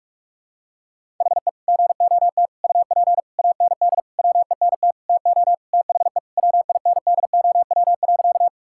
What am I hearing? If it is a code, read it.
Morse code, heard as a keyed tone, about 46 wpm, 700 Hz. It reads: HE GOT UP AND WENT TO THE WINDOW.
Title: Sentences from Top 1600 Words 45wpm